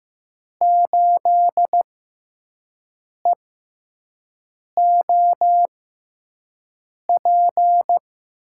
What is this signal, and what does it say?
Morse code, heard as a keyed tone, about 15 wpm, spaced as for 5 wpm, 700 Hz. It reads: 8EOP
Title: USA Call Signs - 5 Characters - 6x Character Spacing - 15wpm